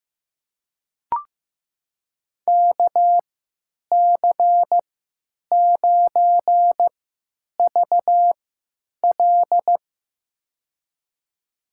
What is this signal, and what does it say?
Morse code, heard as a keyed tone, about 15 wpm, spaced as for 9 wpm, 700 Hz. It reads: KC9VL
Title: USA Call Signs - 5 Characters - 3x Character Spacing - 15wpm